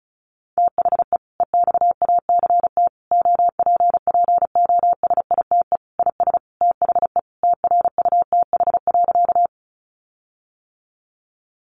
Morse code, heard as a keyed tone, about 34 wpm, 700 Hz. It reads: THE EXACT OPPOSITE IS THE TRUTH.
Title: Sentences from Top 1300 Words 35wpm